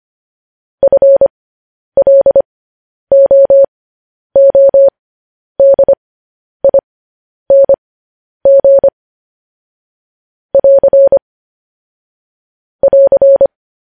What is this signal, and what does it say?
Morse code, heard as a keyed tone, about 25 wpm, spaced as for 10 wpm, 550 Hz. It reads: FLOODING <AR> <AR>